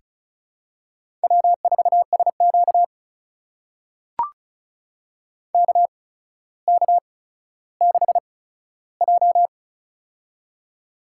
Morse code, heard as a keyed tone, about 36 wpm, spaced as for 15 wpm, 700 Hz. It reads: W4SQ KK6J
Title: USA Call Signs - 4 Characters - 8x Character Spacing - 35wpm